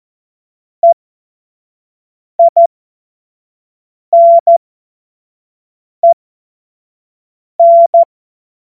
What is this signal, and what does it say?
Morse code, heard as a keyed tone, about 14 wpm, spaced as for 5 wpm, 680 Hz. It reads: EINEN
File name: Witz-09-September-2025-25-F70